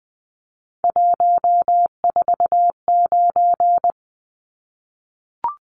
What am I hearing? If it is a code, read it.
Morse code, heard as a keyed tone, about 20 wpm, 700 Hz. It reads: 149
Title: Numbers - 3-Digit 20wpm